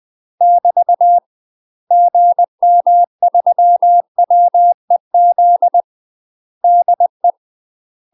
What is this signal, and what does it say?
Morse code, heard as a keyed tone, about 20 wpm, 700 Hz. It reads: <BT> GM3WEZ DE
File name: UK MASTER QSOs TRAINING / 20WPM / UK MASTER QSOs 0044